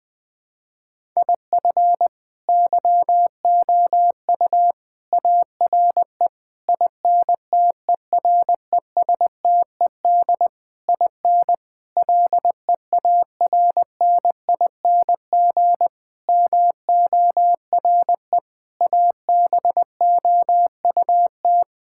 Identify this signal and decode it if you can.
Morse code, heard as a keyed tone, about 19 wpm, 700 Hz. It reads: IF YOU ARE INTERESTED IN LEARNING MORE ABOUT
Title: Sets of 8 Words - 20wpm